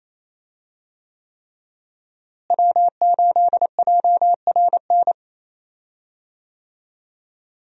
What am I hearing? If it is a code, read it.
Morse code, heard as a keyed tone, about 28 wpm, 700 Hz. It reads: W8JRN
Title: USA Call Signs - 5 Characters - Repeated 3x - 28wpm